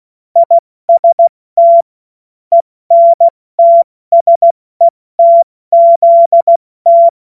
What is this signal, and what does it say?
Morse code, heard as a keyed tone, about 16 wpm, 680 Hz. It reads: IST ENTSETZT
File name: Morsekurs / Witze / Witz 07. Februar 2023 70 F80